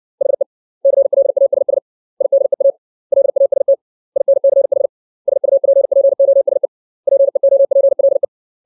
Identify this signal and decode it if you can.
Morse code, heard as a keyed tone, about 60 wpm, 550 Hz. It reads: HE CLASS IDEA LAST INCH SUPPOSE PEOPLE
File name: 632 words / 632 Words 60WPM